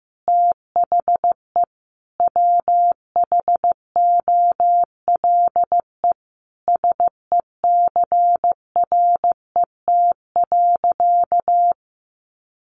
Morse code, heard as a keyed tone, about 15 wpm, 700 Hz. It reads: THE WHOLE SECRET.
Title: Sentences from Top 1200 Words 15wpm